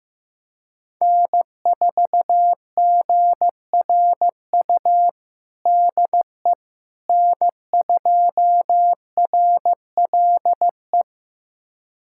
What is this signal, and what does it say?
Morse code, heard as a keyed tone, about 15 wpm, 700 Hz. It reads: N4GRU DE N2RLE
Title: USA Callsign DE Callsign - 5 Characters - 15wpm